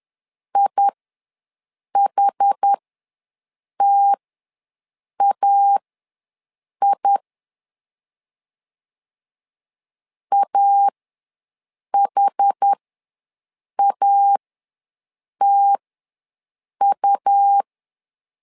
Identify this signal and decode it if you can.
Morse code, heard as a keyed tone, about 10 wpm, spaced as for 6 wpm, 800 Hz. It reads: IHTAI AHATU
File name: LEZIONI / 2 UHDN / 2 UH ws 10wpm 2029